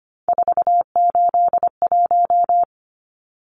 Morse code, heard as a keyed tone, about 25 wpm, 700 Hz. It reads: 481
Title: Numbers - 3-Digit 25wpm